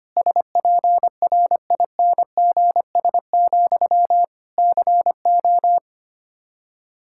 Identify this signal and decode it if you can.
Morse code, heard as a keyed tone, about 25 wpm, 700 Hz. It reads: SPRINGS, CO
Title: Practice QSO - Exchange 1 - Beginning - 25wpm